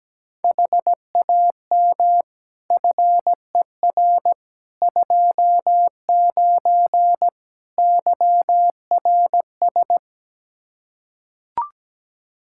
Morse code, heard as a keyed tone, about 17 wpm, 700 Hz. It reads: HAM FER 29 YRS
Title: Practice QSO - Exchange 3 - More Info - 17wpm